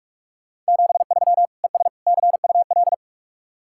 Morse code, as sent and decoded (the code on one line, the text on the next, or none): --... ...-- / . ... / -.-. ..- .-..
73 ES CUL